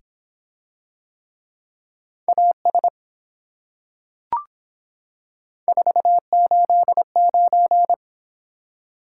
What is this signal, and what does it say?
Morse code, heard as a keyed tone, about 24 wpm, 700 Hz. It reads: AS 489